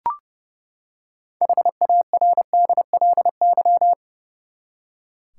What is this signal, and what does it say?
Morse code, heard as a keyed tone, about 30 wpm, 700 Hz. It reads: HARDLY
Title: Six Letter Words - 30wpm